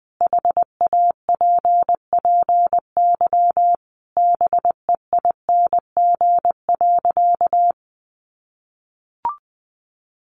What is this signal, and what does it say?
Morse code, heard as a keyed tone, about 20 wpm, 700 Hz. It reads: HAPPY BEING.
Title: Sentences from Top 1000 Words 20wpm